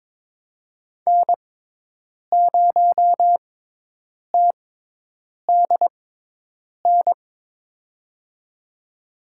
Morse code, heard as a keyed tone, about 21 wpm, spaced as for 8 wpm, 700 Hz. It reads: N0TDN
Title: USA Call Signs - 5 Characters - 6x Character Spacing - 22wpm